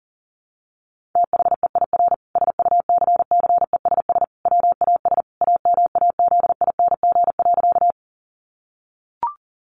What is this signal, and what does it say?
Morse code, heard as a keyed tone, about 40 wpm, 700 Hz. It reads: THEIR SUCCESS WAS AMAZING.
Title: Sentences from Top 1500 Words 40wpm